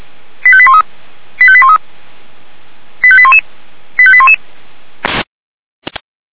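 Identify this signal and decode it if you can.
Five-tone selective call (ZVEI), heard as quick runs of five tones, tones of about 70 ms. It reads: 85813 85813 85811 85811
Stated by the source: Five-tone ZVEI selective call 85813, received by SDR.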